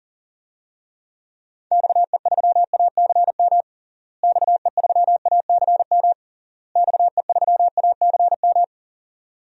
Morse code, heard as a keyed tone, about 38 wpm, 700 Hz. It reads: XE3ACM XE3ACM XE3ACM
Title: DXCC-100 Call Signs - 6 Characters - Repeated 3x - 40wpm